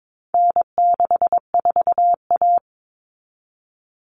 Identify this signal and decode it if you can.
Morse code, heard as a keyed tone, about 22 wpm, 700 Hz. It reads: N64A
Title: Common Cut Numbers - 4-Digit 22wpm